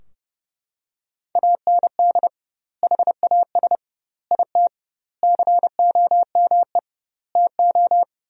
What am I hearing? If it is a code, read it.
Morse code, heard as a keyed tone, about 30 wpm, 700 Hz. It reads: AND HAS IT COME TO
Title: 2x Word-Spacing - Sentences from Top 100 Words - 30wpm